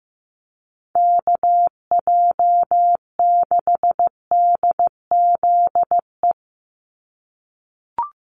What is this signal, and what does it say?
Morse code, heard as a keyed tone, about 15 wpm, 700 Hz. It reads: KJ6DZE
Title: USA Call Signs 15wpm